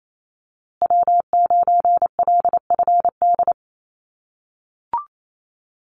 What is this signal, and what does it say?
Morse code, heard as a keyed tone, about 28 wpm, 700 Hz. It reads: W9LFD
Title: USA Call Signs Repeated 2x 28wpm